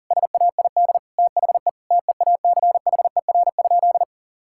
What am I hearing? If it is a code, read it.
Morse code, heard as a keyed tone, about 40 wpm, 700 Hz. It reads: SAID THE TEACHER?
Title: Sentences from Top 1000 Words - Review Part 6 - 40wpm